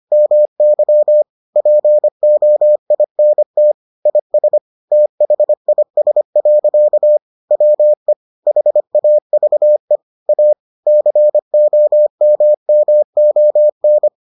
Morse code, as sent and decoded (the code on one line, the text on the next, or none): -- -.-- / .--. --- .. -. - / .. ... / - .... .. ... .-.-.- / .-- . / .... .- ...- . / .- / -.-. --- -- -- --- -.
MY POINT IS THIS. WE HAVE A COMMON